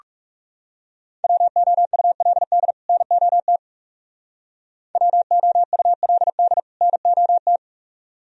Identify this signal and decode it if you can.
Morse code, heard as a keyed tone, about 41 wpm, 700 Hz. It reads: WOULD NOT WOULD NOT